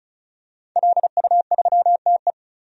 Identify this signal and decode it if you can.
Morse code, heard as a keyed tone, about 34 wpm, 700 Hz. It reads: LU3TE
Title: DXCC-100 Call Signs - 5 Characters - 35wpm